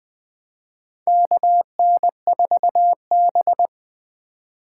Morse code, heard as a keyed tone, about 20 wpm, 700 Hz. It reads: KN4B
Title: USA Call Signs - 4 Characters - 2x Character Spacing - 20wpm